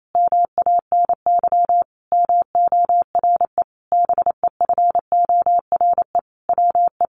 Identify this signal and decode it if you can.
Morse code, heard as a keyed tone, about 28 wpm, 700 Hz. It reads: MANY MORE BEFORE WE
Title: Sentences from Top 1500 Words 28wpm